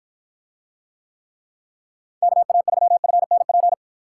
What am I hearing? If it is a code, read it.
Morse code, heard as a keyed tone, about 51 wpm, 700 Hz. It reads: KA3FNP